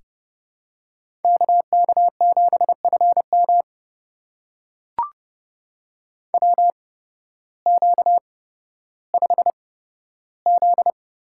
Morse code, heard as a keyed tone, about 30 wpm, spaced as for 14 wpm, 700 Hz. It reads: KK7FM WQ5Z
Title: USA Call Signs - 8x Character Spacing - 30wpm